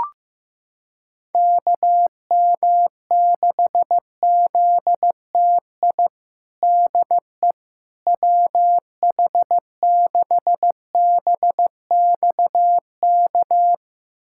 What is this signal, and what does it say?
Morse code, heard as a keyed tone, about 16 wpm, 700 Hz. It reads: KM6ZTI DE WH6BXK
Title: USA Callsign DE Callsign - 6 Characters - 15wpm